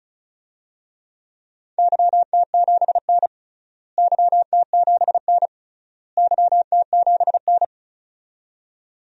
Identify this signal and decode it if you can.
Morse code, heard as a keyed tone, about 36 wpm, 700 Hz. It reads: YT7N YT7N YT7N
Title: DXCC-100 Call Signs - 4 Characters - Repeated 3x - 35wpm